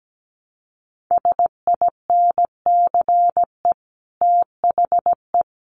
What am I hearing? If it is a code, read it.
Morse code, heard as a keyed tone, about 17 wpm, 700 Hz. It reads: SINCE THE
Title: Sets of 5 Words Encore 17wpm